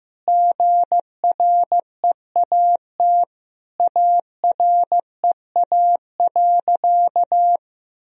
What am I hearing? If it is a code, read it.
Morse code, heard as a keyed tone, about 15 wpm, 700 Hz. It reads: GREAT AREA.
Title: Sentences from Top 1000 Words - Review Part 2 - 15wpm